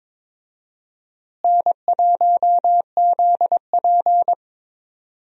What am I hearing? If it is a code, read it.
Morse code, heard as a keyed tone, about 22 wpm, 700 Hz. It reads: N1ZP
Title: USA Call Signs - 4 Characters - Repeated 3x - 22wpm